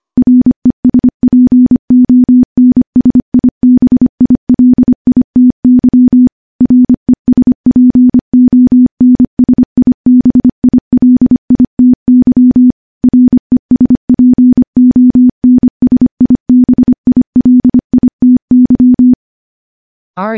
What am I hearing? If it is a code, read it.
Morse code, heard as a keyed tone, about 25 wpm, 250 Hz. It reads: RESPONSIBILITY RESPONSIBILITY RESPONSIBILITY
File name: morse / morse-training-25WPM